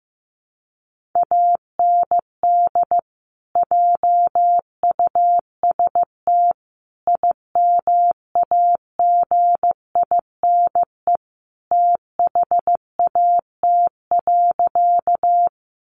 Morse code, heard as a keyed tone, about 15 wpm, 700 Hz. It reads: AND JUST IMAGINE THAT.
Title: Sentences from Top 1400 Words 15wpm